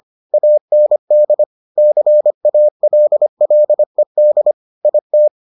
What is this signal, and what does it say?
Morse code, heard as a keyed tone, about 25 wpm, 600 Hz. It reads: AND CALLED IT